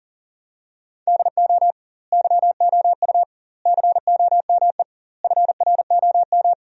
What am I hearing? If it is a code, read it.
Morse code, heard as a keyed tone, about 40 wpm, 700 Hz. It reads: DO YOU COME FROM
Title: 2x Word-Spacing - Sentences from Top 200 Words - 40wpm